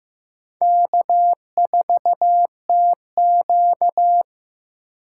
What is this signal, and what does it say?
Morse code, heard as a keyed tone, about 18 wpm, 700 Hz. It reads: K4TQ
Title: Speed Racing - USA 1x2 Call Signs 15wpm